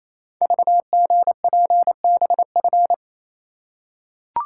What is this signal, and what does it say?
Morse code, heard as a keyed tone, about 27 wpm, 700 Hz. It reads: VGPBF
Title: Code Groups - 5 Letters - 28wpm